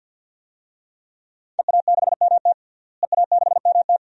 Morse code, heard as a keyed tone, about 53 wpm, 700 Hz. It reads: EA6MT EA6MT
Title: DXCC-100 Call Signs - 5 Characters - Repeated 2x - 50wpm